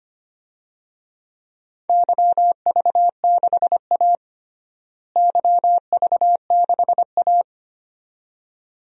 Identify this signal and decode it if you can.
Morse code, heard as a keyed tone, about 25 wpm, 700 Hz. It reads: YV6A YV6A
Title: DXCC-100 Call Signs - 4 Characters - Repeated 2x - 25wpm